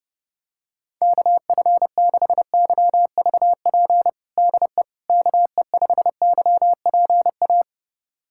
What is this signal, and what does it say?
Morse code, heard as a keyed tone, about 32 wpm, 700 Hz. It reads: KF6YVP DE KE5YPA
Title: USA Callsign DE Callsign 30wpm